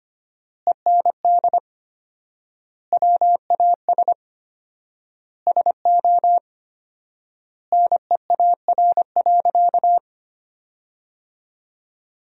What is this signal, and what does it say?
Morse code, heard as a keyed tone, about 25 wpm, 700 Hz. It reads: END WAS SO NEAR.
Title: 4x Word-Spacing - Sentences from Top 300 Words - 25wpm